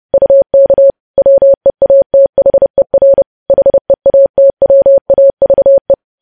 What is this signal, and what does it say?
Morse code, heard as a keyed tone, about 30 wpm, 550 Hz. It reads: UK WEATHER HEATWAVE
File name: podcast / News headlines-1x-30x30